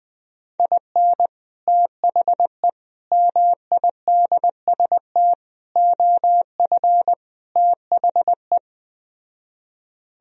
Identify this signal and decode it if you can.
Morse code, heard as a keyed tone, about 20 wpm, 700 Hz. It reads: IN THE MIDST OF THE
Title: Speed-Racing - Sets of 5 Words 17wpm